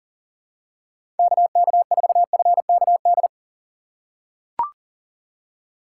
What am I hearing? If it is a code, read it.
Morse code, heard as a keyed tone, about 40 wpm, 700 Hz. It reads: KK4FKD
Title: USA Call Signs - 4x Character Spacing - 40wpm